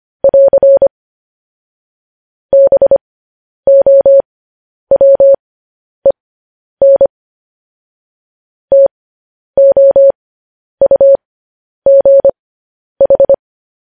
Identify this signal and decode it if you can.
Morse code, heard as a keyed tone, about 25 wpm, spaced as for 10 wpm, 550 Hz. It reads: <AR> BOWEN TOUGH